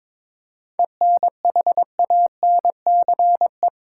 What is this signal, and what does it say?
Morse code, heard as a keyed tone, about 22 wpm, 700 Hz. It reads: ENHANCE
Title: Seven Letter Words - 22wpm